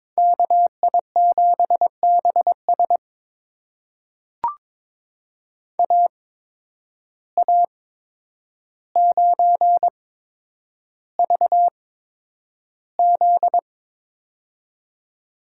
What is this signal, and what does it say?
Morse code, heard as a keyed tone, about 22 wpm, spaced as for 9 wpm, 700 Hz. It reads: KI7BS AA9VZ
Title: USA Call Signs - 5 Characters - 8x Character Spacing - 22wpm